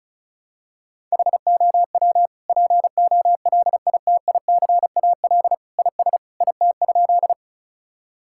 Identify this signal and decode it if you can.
Morse code, heard as a keyed tone, about 36 wpm, 700 Hz. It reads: HOW POLITICAL IS IT?
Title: Sentences from Top 1000 Words - Review Part 5 - 35wpm